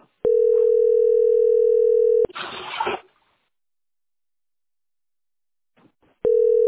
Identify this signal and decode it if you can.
Ringback tone (North America), 440 + 480 Hz.